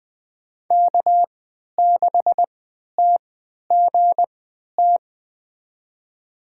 Morse code, heard as a keyed tone, about 20 wpm, spaced as for 11 wpm, 700 Hz. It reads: K6TGT
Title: USA Call Signs - 5 Characters - 3x Character Spacing - 20wpm